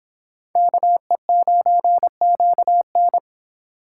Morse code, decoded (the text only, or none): KE9QN